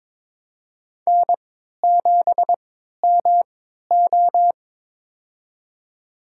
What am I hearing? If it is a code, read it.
Morse code, heard as a keyed tone, about 22 wpm, spaced as for 13 wpm, 700 Hz. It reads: N7MO
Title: USA Call Signs - 4 Characters - 3x Character Spacing - 22wpm